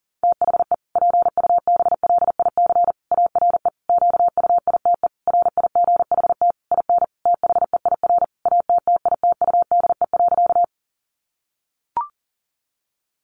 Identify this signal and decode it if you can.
Morse code, heard as a keyed tone, about 40 wpm, 700 Hz. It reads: THE PUBLIC ARE QUITE RIGHT IN THEIR ATTITUDE.
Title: Sentences from Top 1300 Words 40wpm